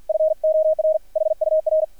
Morse code, heard as a keyed tone, about 50 wpm, 630 Hz. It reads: FOR HUR
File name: archived / CRSnet